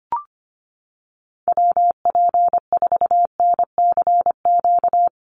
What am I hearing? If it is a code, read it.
Morse code, heard as a keyed tone, about 25 wpm, 700 Hz. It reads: WP4NCQ
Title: USA Call Signs Repeated 2x 25wpm